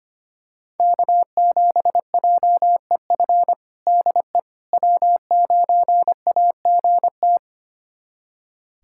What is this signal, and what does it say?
Morse code, heard as a keyed tone, about 25 wpm, 700 Hz. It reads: K7JEF DE W9AGT
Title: USA Callsign DE Callsign - 5 Characters - 25wpm